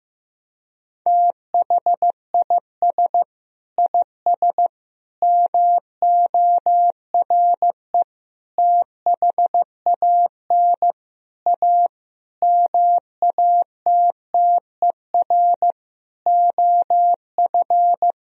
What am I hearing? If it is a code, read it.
Morse code, heard as a keyed tone, about 15 wpm, 700 Hz. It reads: THIS IS MORE THAN A MATTER OF
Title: Sentences from Top 1000 Words - Review Part 3 - 15wpm